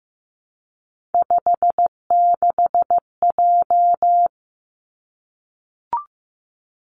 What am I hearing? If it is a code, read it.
Morse code, heard as a keyed tone, about 15 wpm, 700 Hz. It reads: 56J
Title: Three Challenging Letter-Number Combinations 15wpm